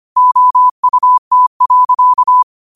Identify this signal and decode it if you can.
Morse code, heard as a keyed tone, about 25 wpm, 1000 Hz. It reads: OUT.
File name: qotdfiles / 25wpm